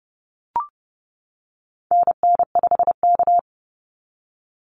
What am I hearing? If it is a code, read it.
Morse code, heard as a keyed tone, about 30 wpm, 700 Hz. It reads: NN5K
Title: NA SOTA Activator Callsigns 30wpm